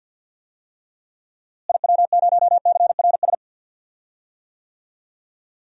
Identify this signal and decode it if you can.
Morse code, heard as a keyed tone, about 47 wpm, 700 Hz. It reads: IW0CRS
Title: DXCC-100 Call Signs - 6 Characters - Repeated 3x - 50wpm